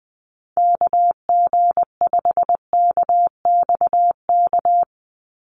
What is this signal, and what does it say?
Morse code, heard as a keyed tone, about 20 wpm, 700 Hz. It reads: KG5KXK
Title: USA Call Signs 20wpm